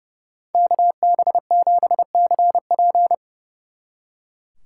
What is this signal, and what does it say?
Morse code, heard as a keyed tone, about 35 wpm, 700 Hz. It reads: KB7CP
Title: Speed Racing - USA 2x2 Call Signs 30wpm